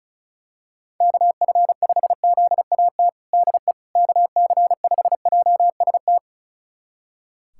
Morse code, heard as a keyed tone, about 34 wpm, 700 Hz. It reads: KF5ZAT DE KC5JST
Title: USA Callsign DE Callsign - 6 Characters - 35wpm